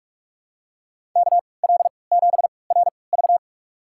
Morse code, heard as a keyed tone, about 47 wpm, spaced as for 26 wpm, 700 Hz. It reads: KL7RV
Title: USA Call Signs - 3x Character Spacing - 45wpm